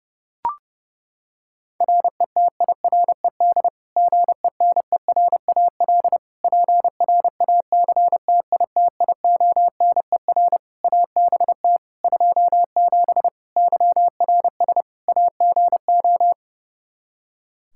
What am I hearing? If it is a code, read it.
Morse code, heard as a keyed tone, about 30 wpm, 700 Hz. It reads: RETIRED GENERAL PRACTITIONER ABT 27 YRS AGO
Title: Practice QSO - Exchange 3 - More Info - 30wpm